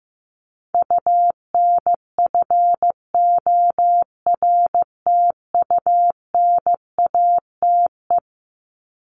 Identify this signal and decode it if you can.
Morse code, heard as a keyed tone, about 15 wpm, 700 Hz. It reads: UNFORTUNATE
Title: LY Suffix - Top 100 Words with Base Words 15wpm